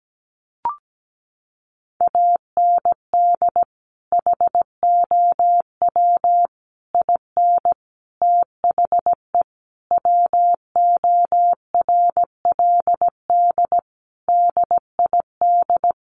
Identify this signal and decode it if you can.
Morse code, heard as a keyed tone, about 17 wpm, 700 Hz. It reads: AND HOW IN THE WORLD DID
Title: Sentences from Top 200 Words 17wpm